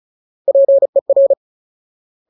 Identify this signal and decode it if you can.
Morse code, heard as a keyed tone, about 36 wpm, 550 Hz. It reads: PER
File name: Ultimate Word List / 35WPM Ultimate